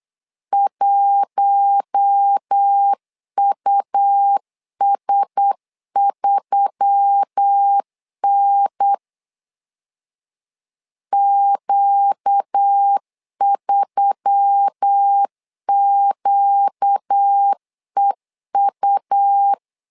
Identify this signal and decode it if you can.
Morse code, heard as a keyed tone, about 8 wpm, 800 Hz. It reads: 1US3N Q3QEU
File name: LEZIONI / 5 LBVC / 5 L 8wpm 2831